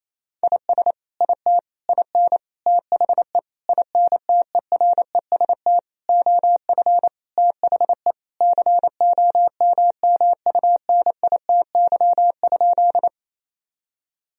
Morse code, heard as a keyed tone, about 28 wpm, 700 Hz. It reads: IS IT IN THE INTEREST OF THE COMMUNITY?
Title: Sentences from Top 1000 Words - Review Part 4 - 28wpm